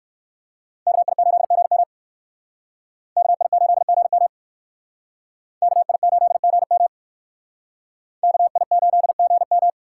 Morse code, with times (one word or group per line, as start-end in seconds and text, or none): 0.9-1.8 KI8GM
3.2-4.3 KI8GM
5.6-6.9 KI8GM
8.2-9.7 KI8GM